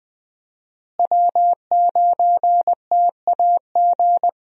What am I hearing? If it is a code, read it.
Morse code, heard as a keyed tone, about 20 wpm, 700 Hz. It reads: W9TAG
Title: USA Call Signs - 5 Characters - Repeated 3x - 20wpm